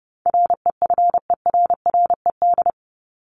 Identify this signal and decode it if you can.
Morse code, heard as a keyed tone, about 30 wpm, 700 Hz. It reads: REFERRED
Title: ED Suffix - Top 100 Words with Base Words 30wpm